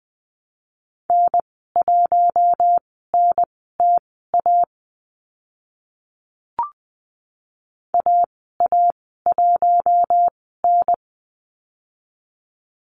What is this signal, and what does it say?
Morse code, heard as a keyed tone, about 20 wpm, spaced as for 15 wpm, 700 Hz. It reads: N1NTA AA1N
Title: 042 Lesson 8g - Number 1 - Call Signs 20wpm